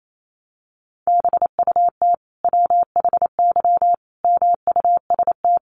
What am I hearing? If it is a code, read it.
Morse code, heard as a keyed tone, about 28 wpm, 700 Hz. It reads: BUT WHY MUST
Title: Sentences from Top 400 Words 28wpm